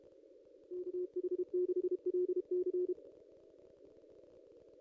Morse code, heard as a keyed tone, about 32 wpm, 360 Hz. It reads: KH6LC
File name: contests / ss-cw-k5tr-op-n5rz -0900